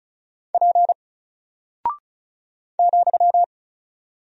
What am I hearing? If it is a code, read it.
Morse code, heard as a keyed tone, about 34 wpm, 700 Hz. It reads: P ,
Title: Single Character (Letter:Number:Punctuation:Prosign) Alt 35wpm